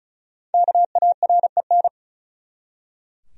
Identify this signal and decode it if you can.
Morse code, heard as a keyed tone, about 35 wpm, 700 Hz. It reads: KAREN
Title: USA Common Names - Max 5 Letters - 35wpm